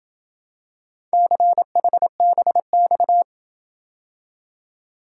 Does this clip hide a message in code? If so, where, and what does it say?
Yes: Morse code from 1.1 to 3.2 s, heard as a keyed tone, about 27 wpm, 700 Hz. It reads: CHBX